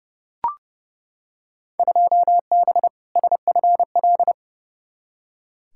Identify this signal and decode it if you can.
Morse code, heard as a keyed tone, about 30 wpm, 700 Hz. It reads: 2B SFL
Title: Field Day Exchanges Encore - 30wpm